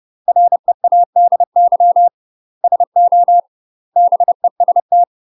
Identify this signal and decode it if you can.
Morse code, heard as a keyed tone, about 30 wpm, 700 Hz. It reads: READY SO BEST
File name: UK MASTER QSOs TRAINING / 30WPM / UK MASTER QSOs 0045